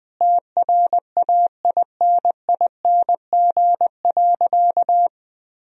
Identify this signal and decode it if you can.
Morse code, heard as a keyed tone, about 20 wpm, 700 Hz. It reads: TRAINING.
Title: Sentences from Top 1000 Words - Review Part 4 - 20wpm